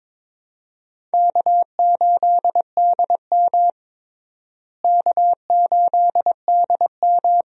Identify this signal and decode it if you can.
Morse code, heard as a keyed tone, about 22 wpm, 700 Hz. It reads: K8DM K8DM
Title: USA Call Signs - 4 Characters - Repeated 2x - 22wpm